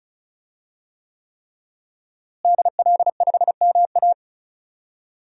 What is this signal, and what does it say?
Morse code, heard as a keyed tone, about 36 wpm, 700 Hz. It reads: DL5MA
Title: DXCC-100 Call Signs - 5 Characters - Repeated 3x - 35wpm